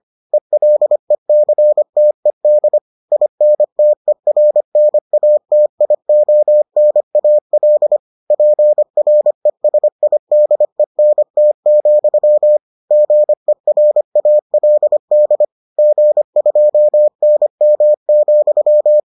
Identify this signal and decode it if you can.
Morse code, heard as a keyed tone, about 25 wpm, 600 Hz. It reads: ELECTED INTERNATIONAL PRESIDENT, GERALD G2NM,